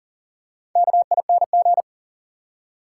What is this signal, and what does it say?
Morse code, heard as a keyed tone, about 40 wpm, 700 Hz. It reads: KING
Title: Four Letter Words - 40wpm